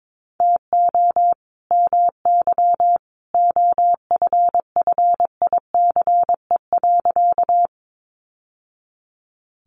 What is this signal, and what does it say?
Morse code, heard as a keyed tone, about 22 wpm, 700 Hz. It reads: TO MY OFFICE.
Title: Sentences from Top 1000 Words 22wpm